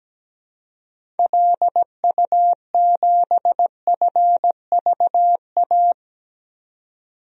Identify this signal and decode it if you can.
Morse code, heard as a keyed tone, about 17 wpm, 700 Hz. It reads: LU7FVA
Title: DXCC-100 Call Signs - 6 Characters - 17wpm